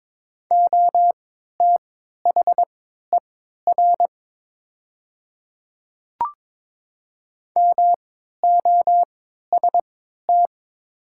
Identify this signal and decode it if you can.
Morse code, heard as a keyed tone, about 22 wpm, spaced as for 12 wpm, 700 Hz. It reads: OTHER MOST
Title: 3x Character Spacing - Top 100 Words - 22wpm